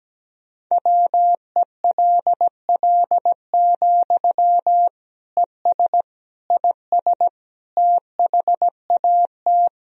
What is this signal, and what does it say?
Morse code, heard as a keyed tone, about 17 wpm, 700 Hz. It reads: WELL, ES IS THAT
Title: Sentences from Top 200 Words and Abbreviations 17wpm